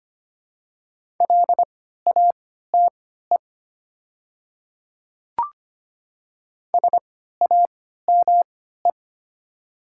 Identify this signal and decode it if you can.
Morse code, heard as a keyed tone, about 25 wpm, spaced as for 14 wpm, 700 Hz. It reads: LATE SAME
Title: 3x Character Spacing - Top 500 Words - 25wpm